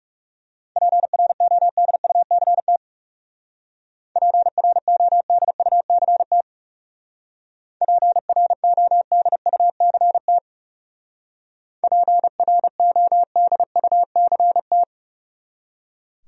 Morse code, heard as a keyed tone, about 37 wpm, 700 Hz. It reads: PRODUCT PRODUCT PRODUCT PRODUCT
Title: Speed-Racing Top 500 Words 30wpm